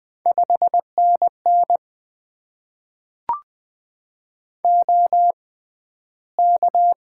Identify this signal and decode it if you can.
Morse code, heard as a keyed tone, about 20 wpm, spaced as for 12 wpm, 700 Hz. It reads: 5NN OK
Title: Top 90 QSO Elements - 6x Character Spacing - 20wpm